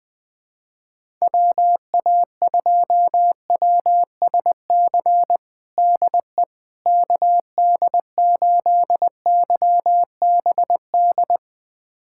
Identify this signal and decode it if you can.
Morse code, heard as a keyed tone, about 20 wpm, 700 Hz. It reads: WA2WSC DE KD8YBD
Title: USA Callsign DE Callsign - 6 Characters - 20wpm